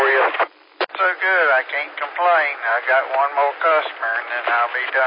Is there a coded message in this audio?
Yes: Emergency Alert System (SAME) tones from 1.1 to 4.4 s, heard as harsh buzzing bursts, with speech.